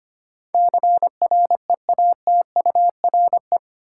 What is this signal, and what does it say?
Morse code, heard as a keyed tone, about 25 wpm, 700 Hz. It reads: CREATURE
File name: 8 Letter Words - First 3 Letters Given - 25wpm